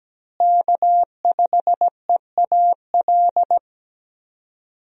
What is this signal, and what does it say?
Morse code, heard as a keyed tone, about 17 wpm, 700 Hz. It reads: K5EAL
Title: USA Call Signs - 5 Characters - Repeated 3x - 17wpm